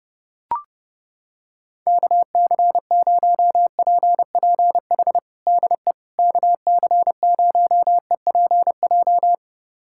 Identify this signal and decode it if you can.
Morse code, heard as a keyed tone, about 32 wpm, 700 Hz. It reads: KC0PPH DE KC0EPJ
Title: USA Callsign DE Callsign 30wpm